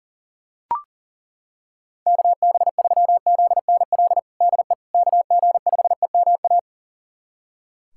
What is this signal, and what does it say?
Morse code, heard as a keyed tone, about 40 wpm, 700 Hz. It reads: KB3ZNL DE KG5EMA
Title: USA Callsign DE Callsign - 6 Characters - 40wpm